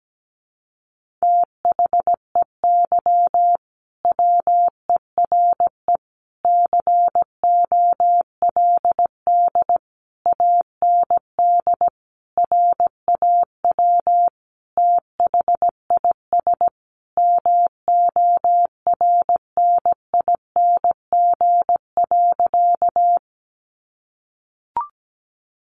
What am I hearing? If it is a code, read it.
Morse code, heard as a keyed tone, about 17 wpm, 700 Hz. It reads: THEY WERE COLD AND RAW THIS MORNING.
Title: Sentences from Top 1500 Words 17wpm